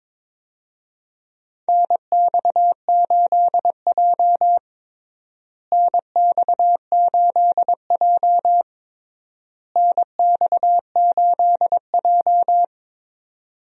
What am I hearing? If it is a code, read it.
Morse code, heard as a keyed tone, about 22 wpm, 700 Hz. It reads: NX8J NX8J NX8J
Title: USA Call Signs - 4 Characters - Repeated 3x - 22wpm